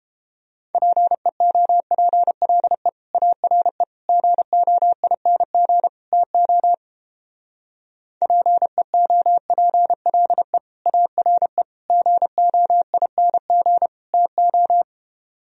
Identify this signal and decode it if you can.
Morse code, heard as a keyed tone, about 29 wpm, 700 Hz. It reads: PEOPLE ARE GOING TO PEOPLE ARE GOING TO